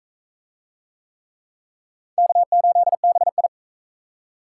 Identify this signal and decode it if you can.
Morse code, heard as a keyed tone, about 38 wpm, 700 Hz. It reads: K8BI